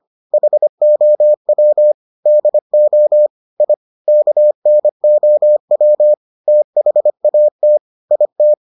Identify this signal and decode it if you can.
Morse code, heard as a keyed tone, about 25 wpm, 600 Hz. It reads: HOW DO I KNOW THAT IT